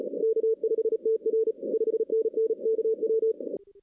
Morse code, heard as a keyed tone, about 32 wpm, 500 Hz. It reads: K5TR 5NN KW
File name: 2016-ARRL-DX-CW-K5TR